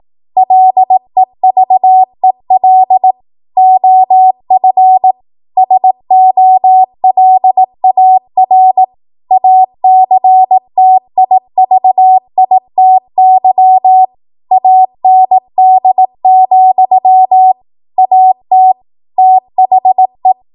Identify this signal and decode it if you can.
Morse code, heard as a keyed tone, about 18 wpm, 750 Hz. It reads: LEVEL OF SOLAR ACTIVITY AND, AT THE
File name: Morse / 18WPM